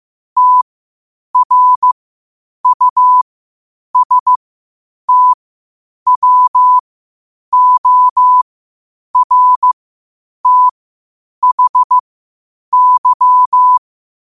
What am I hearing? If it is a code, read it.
Morse code, heard as a keyed tone, about 15 wpm, spaced as for 8 wpm, 1000 Hz. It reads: TRUSTWORTHY
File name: qotdfiles / 8wpm